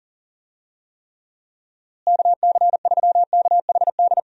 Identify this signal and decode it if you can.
Morse code, heard as a keyed tone, about 40 wpm, 700 Hz. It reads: KC3KHD
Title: USA Call Signs - 6 Characters - Repeated 2x - 40wpm